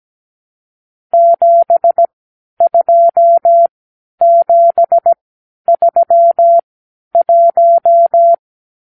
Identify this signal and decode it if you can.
Morse code, heard as a keyed tone, about 17 wpm, spaced as for 10 wpm, 680 Hz. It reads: 72731
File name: arrl / Review (1-0)